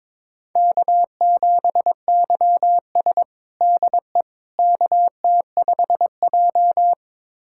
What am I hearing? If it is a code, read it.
Morse code, heard as a keyed tone, about 22 wpm, 700 Hz. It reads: K7YS DE KT5J
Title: USA Callsign DE Callsign - 4 Characters - 22wpm